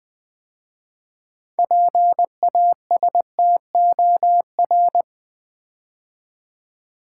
Morse code, heard as a keyed tone, about 20 wpm, 700 Hz. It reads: PASTOR
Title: 6 Letter Words - First 3 Letters Given - 20wpm